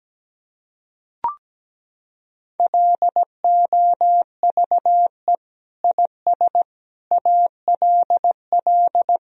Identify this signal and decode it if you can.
Morse code, heard as a keyed tone, about 17 wpm, 700 Hz. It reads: LOVE IS ALL
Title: Sentences from Top 1000 Words - Review Part 3 - 17wpm